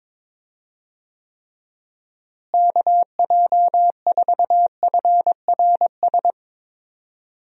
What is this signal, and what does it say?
Morse code, heard as a keyed tone, about 22 wpm, 700 Hz. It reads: KJ4FRS
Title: USA Call Signs - 6 Characters - Repeated 3x - 22wpm